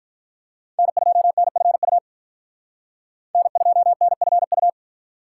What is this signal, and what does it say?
Morse code, heard as a keyed tone, about 52 wpm, 700 Hz. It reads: N2NFU N2NFU